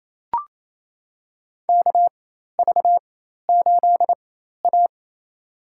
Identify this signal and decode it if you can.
Morse code, heard as a keyed tone, about 28 wpm, spaced as for 13 wpm, 700 Hz. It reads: KV8A
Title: USA Call Signs - 4 Characters - 4x Character Spacing - 28wpm